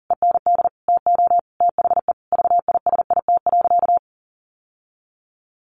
Morse code, heard as a keyed tone, about 40 wpm, 700 Hz. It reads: END TO THE VISIT.
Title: Sentences from Top 1200 Words 40wpm